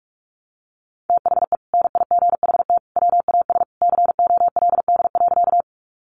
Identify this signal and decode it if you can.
Morse code, heard as a keyed tone, about 44 wpm, 700 Hz. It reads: THE NIGHT WAS COLD.
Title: Sentences from Top 400 Words 45wpm